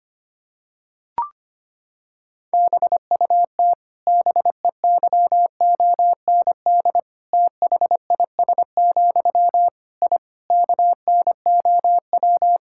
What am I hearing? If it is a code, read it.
Morse code, heard as a keyed tone, about 25 wpm, 700 Hz. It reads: BUT BEYOND THIS, I KNOW
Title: Sentences from Top 1000 Words - Review Part 6 - 25wpm